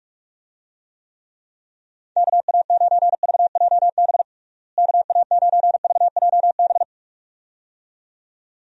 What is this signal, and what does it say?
Morse code, heard as a keyed tone, about 46 wpm, 700 Hz. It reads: KA9VJB KA9VJB
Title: USA Call Signs - 6 Characters - Repeated 2x - 45wpm